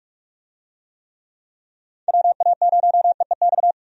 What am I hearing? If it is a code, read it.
Morse code, heard as a keyed tone, about 46 wpm, 700 Hz. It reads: WA0EEX
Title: USA Call Signs - 6 Characters - 45wpm